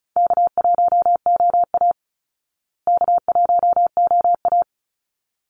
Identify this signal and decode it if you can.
Morse code, heard as a keyed tone, about 34 wpm, 700 Hz. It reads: K1OA K1OA
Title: Call Signs Repeated 3x - 35wpm